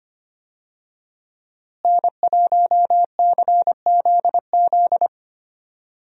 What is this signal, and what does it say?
Morse code, heard as a keyed tone, about 25 wpm, 700 Hz. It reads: N1CZZ
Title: USA Call Signs - 5 Characters - Repeated 3x - 25wpm